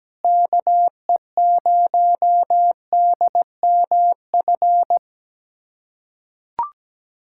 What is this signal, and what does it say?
Morse code, heard as a keyed tone, about 17 wpm, 700 Hz. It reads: KE0DMF
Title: USA Callsign DE Callsign 17wpm